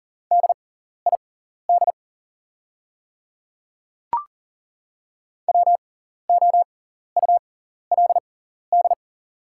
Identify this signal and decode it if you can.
Morse code, heard as a keyed tone, about 40 wpm, spaced as for 14 wpm, 690 Hz. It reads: DID WOULD
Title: 6x Character Spacing - Top 100 Words - 40wpm